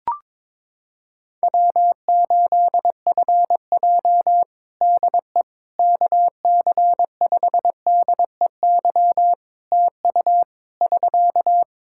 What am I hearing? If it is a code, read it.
Morse code, heard as a keyed tone, about 22 wpm, 700 Hz. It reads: W8FJ DE KC5DEY TU <SK>
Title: Practice QSO - Exchange 4 - Ending - 22wpm